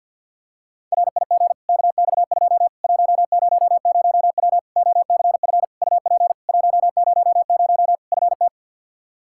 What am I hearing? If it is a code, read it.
Morse code, heard as a keyed tone, about 46 wpm, 700 Hz. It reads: RIG KX2 100W OCF UP 100 FT
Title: Practice QSO - Exchange 2 - Info - 50wpm